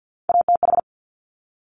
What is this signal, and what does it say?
Morse code, heard as a keyed tone, about 50 wpm, 700 Hz. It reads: ATH
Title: Top Three Letter Combinations in Words 50wpm